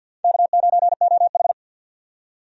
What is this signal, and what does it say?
Morse code, heard as a keyed tone, about 52 wpm, 700 Hz. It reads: K9OH
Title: USA Call Signs - 4 Characters - 50wpm